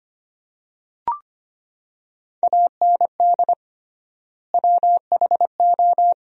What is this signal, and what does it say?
Morse code, heard as a keyed tone, about 25 wpm, 700 Hz. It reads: AND WHO
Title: 3x Word-Spacing - Sentences from Top 100 Words - 25wpm